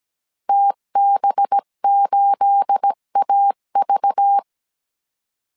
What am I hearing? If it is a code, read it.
Morse code, heard as a keyed tone, about 17 wpm, 800 Hz. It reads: TB8AV
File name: CW / DET / 1 9 1500C S B 16wpm 2856